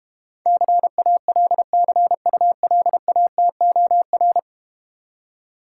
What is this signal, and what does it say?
Morse code, heard as a keyed tone, about 31 wpm, 700 Hz. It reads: CALCULATOR